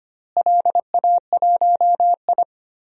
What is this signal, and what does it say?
Morse code, heard as a keyed tone, about 25 wpm, 700 Hz. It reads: LA1I
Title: DXCC-100 Call Signs - 4 Characters - Repeated 3x - 25wpm